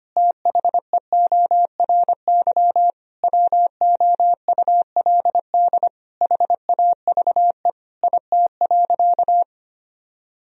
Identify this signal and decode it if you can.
Morse code, heard as a keyed tone, about 25 wpm, 700 Hz. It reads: THEORY WOULD HAVE IT.
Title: Sentences from Top 1000 Words - Review Part 3 - 25wpm